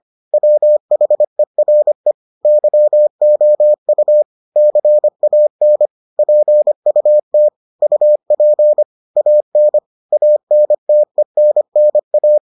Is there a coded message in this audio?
Yes: Morse code from 0.3 to 12.4 s, heard as a keyed tone, about 25 wpm, 600 Hz. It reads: WHERE YOU CAN PUT UP AN ANTENNA